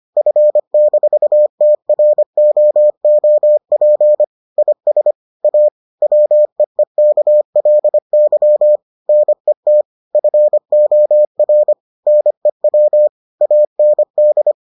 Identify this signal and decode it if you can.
Morse code, heard as a keyed tone, about 25 wpm, 600 Hz. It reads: F-TROOP IS A WEEKLY NET FOR NEW AND